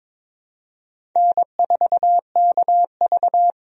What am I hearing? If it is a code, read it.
Morse code, heard as a keyed tone, about 22 wpm, 700 Hz. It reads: N4KV
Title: USA Call Signs - 4 Characters - 22wpm